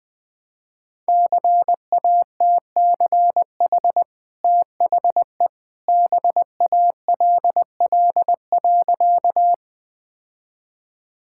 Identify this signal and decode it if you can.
Morse code, heard as a keyed tone, about 20 wpm, 700 Hz. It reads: CATCH THE BALL.
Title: Easy Sentences with Hints - 20wpm